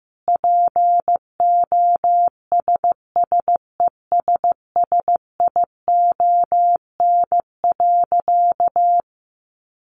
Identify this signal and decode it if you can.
Morse code, heard as a keyed tone, about 14 wpm, 700 Hz. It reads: POSSESSION.
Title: Sentences from Top 1400 Words 15wpm